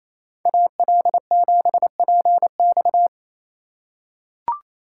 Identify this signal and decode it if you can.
Morse code, heard as a keyed tone, about 28 wpm, 700 Hz. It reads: AL7PX
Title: USA Call Signs - 5 Characters - 4x Character Spacing - 28wpm